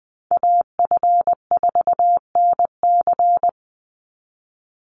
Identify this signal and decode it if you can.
Morse code, heard as a keyed tone, about 20 wpm, 700 Hz. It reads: AF4NC
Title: Call Signs Repeated 3x - 20wpm